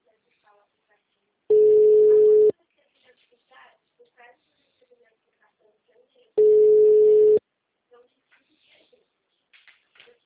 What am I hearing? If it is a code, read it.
Ringback tone (Europe, 425 Hz), with speech.